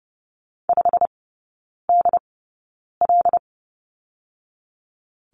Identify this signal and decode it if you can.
Morse code, heard as a keyed tone, about 30 wpm, spaced as for 9 wpm, 690 Hz. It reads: 5DL
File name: Lesson 17e - Letter u - Three Characters 30wpm